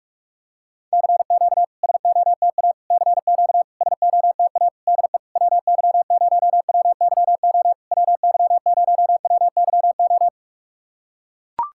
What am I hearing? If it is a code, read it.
Morse code, heard as a keyed tone, about 44 wpm, 690 Hz. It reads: CQ SOTA CQ SOTA DE WY0WYO WY0WYO
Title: SOTA QSO Practice 45wpm